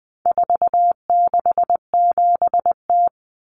Morse code, heard as a keyed tone, about 20 wpm, 700 Hz. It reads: 467T
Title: Common Cut Numbers - 4-Digit 20wpm